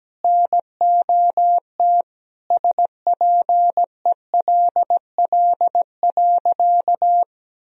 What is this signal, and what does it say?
Morse code, heard as a keyed tone, about 17 wpm, 700 Hz. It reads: NOT SPELL.
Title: Sentences from Top 1000 Words - Review Part 1 - 17wpm